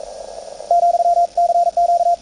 Morse code, heard as a keyed tone, about 41 wpm, 690 Hz, with radio noise. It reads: , CQ
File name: attachments / CW 20m 200Hz